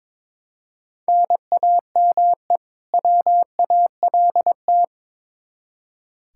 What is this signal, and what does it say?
Morse code, heard as a keyed tone, about 22 wpm, 700 Hz. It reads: NAME WALT
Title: Practice QSO - Exchange 1 - Beginning - 22wpm